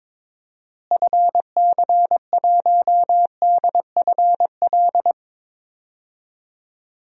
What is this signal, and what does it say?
Morse code, heard as a keyed tone, about 22 wpm, 700 Hz. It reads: FC1DFL
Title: DXCC-100 Call Signs - 6 Characters - Repeated 2x - 22wpm